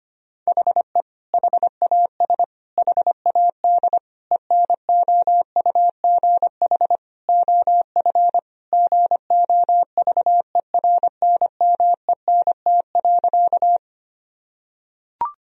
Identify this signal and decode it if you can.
Morse code, heard as a keyed tone, about 25 wpm, 700 Hz. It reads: HE HAS HAD ENOUGH OF GOVERNMENT.
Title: Sentences from Top 1000 Words - Review Part 3 - 25wpm